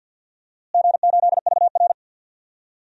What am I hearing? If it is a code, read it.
Morse code, heard as a keyed tone, about 50 wpm, 700 Hz. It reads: G8VR
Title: DXCC-100 Call Signs - 4 Characters - Repeated 3x - 50wpm